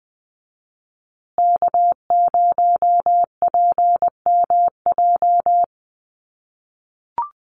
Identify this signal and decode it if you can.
Morse code, heard as a keyed tone, about 20 wpm, 700 Hz. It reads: K0PMJ
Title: USA Call Signs Repeated 3x 20wpm